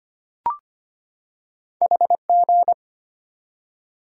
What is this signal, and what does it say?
Morse code, heard as a keyed tone, about 25 wpm, 700 Hz. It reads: HG
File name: Two Letter-Number Combinations - 25wpm